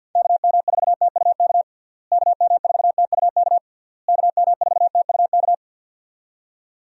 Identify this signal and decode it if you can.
Morse code, heard as a keyed tone, about 52 wpm, 700 Hz. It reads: KM4TUK KM4TUK KM4TUK
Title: USA Call Signs - 6 Characters - Repeated 3x - 50wpm